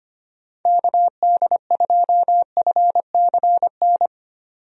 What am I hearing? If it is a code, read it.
Morse code, heard as a keyed tone, about 25 wpm, 700 Hz. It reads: KD2FCN
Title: USA Call Signs - 6 Characters - 25wpm